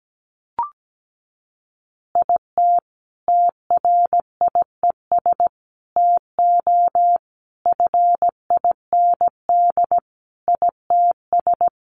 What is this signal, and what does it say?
Morse code, heard as a keyed tone, about 17 wpm, 700 Hz. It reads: IT TRIES TO FIND ITS
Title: Additional Easy Phrases 17wpm